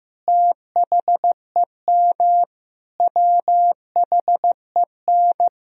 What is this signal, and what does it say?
Morse code, heard as a keyed tone, about 15 wpm, 700 Hz. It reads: THEM WHEN
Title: Sentences from Top 1000 Words - Review Part 1 - 15wpm